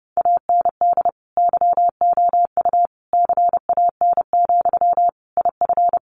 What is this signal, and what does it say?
Morse code, heard as a keyed tone, about 30 wpm, 700 Hz. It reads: AND YOU CAN, IF
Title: Sentences from Top 1500 Words 30wpm